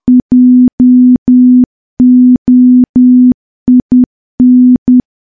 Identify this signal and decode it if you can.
Morse code, heard as a keyed tone, about 10 wpm, 250 Hz. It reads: JOIN
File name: morse / morse-training-10WPM